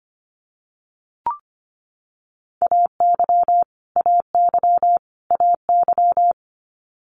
Morse code, heard as a keyed tone, about 25 wpm, 700 Hz. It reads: AY AY AY
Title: Multi-letter Phonemes Repeated 3x 25wpm